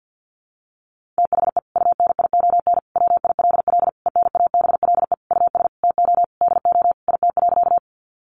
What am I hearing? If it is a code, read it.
Morse code, heard as a keyed tone, about 50 wpm, 700 Hz. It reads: THE UNION WILL ENABLE US TO DO IT.
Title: Sentences from Top 1500 Words 50wpm